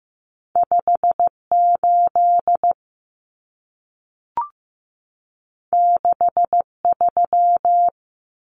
Morse code, heard as a keyed tone, about 15 wpm, 700 Hz. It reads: 58 63
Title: Common Cut Numbers - 2-Digit 15wpm